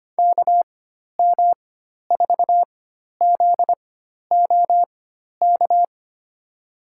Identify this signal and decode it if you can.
Morse code, heard as a keyed tone, about 25 wpm, spaced as for 12 wpm, 700 Hz. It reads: KM4ZOK
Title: USA Call Signs - 4x Character Spacing - 25wpm